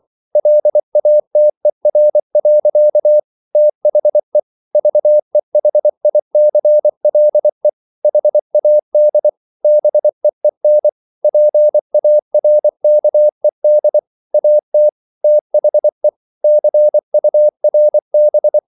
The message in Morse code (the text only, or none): LATER. THE VEHICLE HAD BEEN PARKED AT THE CURB